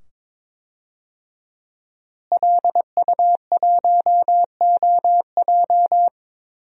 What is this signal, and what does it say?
Morse code, heard as a keyed tone, about 22 wpm, 700 Hz. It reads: LU1OJ
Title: DXCC-100 Call Signs - 5 Characters - Repeated 2x - 22wpm